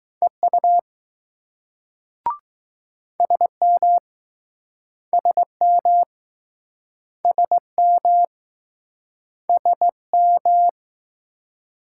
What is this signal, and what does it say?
Morse code, heard as a keyed tone, about 20 wpm, 700 Hz. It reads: EU SM SM SM SM